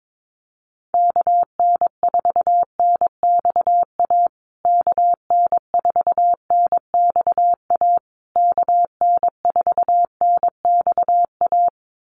Morse code, heard as a keyed tone, about 22 wpm, 700 Hz. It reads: KN4NXA KN4NXA KN4NXA
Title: USA Call Signs Repeated 3x 22wpm